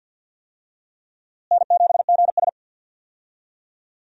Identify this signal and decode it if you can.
Morse code, heard as a keyed tone, about 46 wpm, 700 Hz. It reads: N7GS
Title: USA Call Signs - 4 Characters - Repeated 2x - 50wpm